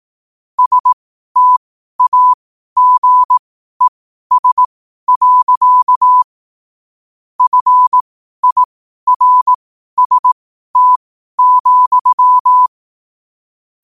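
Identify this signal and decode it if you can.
Morse code, heard as a keyed tone, about 18 wpm, spaced as for 12 wpm, 1000 Hz. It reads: STAGES. FIRST,
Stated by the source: Morse code quote of the day at 12 wpm.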